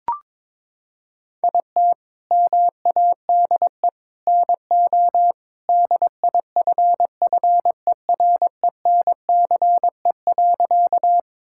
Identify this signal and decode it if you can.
Morse code, heard as a keyed tone, about 22 wpm, 700 Hz. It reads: IT MADE NO DIFFERENCE.
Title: Sentences from Top 1000 Words - Review Part 5 - 22wpm